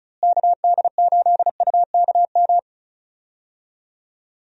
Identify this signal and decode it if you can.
Morse code, heard as a keyed tone, about 34 wpm, 700 Hz. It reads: KD8UKM
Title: USA Call Signs - 6 Characters - Repeated 2x - 35wpm